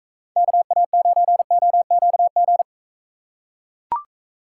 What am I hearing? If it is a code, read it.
Morse code, heard as a keyed tone, about 39 wpm, 700 Hz. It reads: KA9OQG